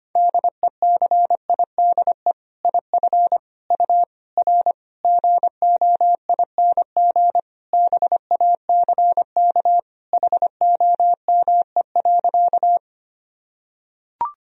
Morse code, heard as a keyed tone, about 25 wpm, 700 Hz. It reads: DECIDE IF U R GOING BACK HOME.
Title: Sentences from Top 500 Words and Abbreviations 25wpm